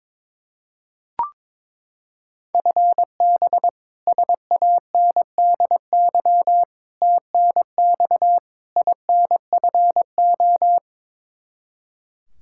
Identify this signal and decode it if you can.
Morse code, heard as a keyed tone, about 22 wpm, 700 Hz. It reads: FB SANDY TNX INFO
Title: Practice QSO - Exchange 3 - More Info - 22wpm